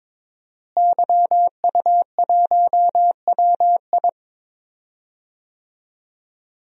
Morse code, heard as a keyed tone, about 22 wpm, 700 Hz. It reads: YU1WI
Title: DXCC-100 Call Signs - 5 Characters - Repeated 2x - 22wpm